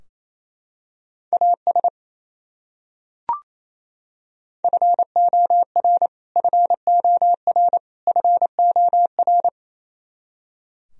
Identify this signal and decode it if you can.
Morse code, heard as a keyed tone, about 27 wpm, 700 Hz. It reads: AS FOR FOR FOR
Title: Top 100 Words Repeated 28wpm